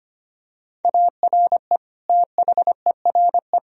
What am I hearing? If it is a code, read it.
Morse code, heard as a keyed tone, about 25 wpm, 700 Hz. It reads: ARE THERE
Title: Sentences from Top 1000 Words - Review Part 4 - 25wpm